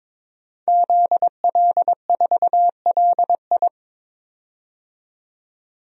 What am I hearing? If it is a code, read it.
Morse code, heard as a keyed tone, about 23 wpm, 700 Hz. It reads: ZL4LI
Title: DXCC-100 Call Signs - 5 Characters - 22wpm